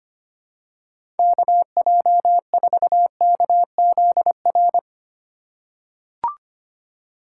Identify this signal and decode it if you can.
Morse code, heard as a keyed tone, about 25 wpm, 700 Hz. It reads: KJ4KZR
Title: USA Call Signs - 6 Characters - 6x Character Spacing - 25wpm